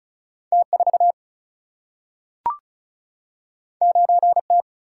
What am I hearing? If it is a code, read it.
Morse code, heard as a keyed tone, about 36 wpm, 700 Hz. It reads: T4 9T
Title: Single-Letter Single-Number 35wpm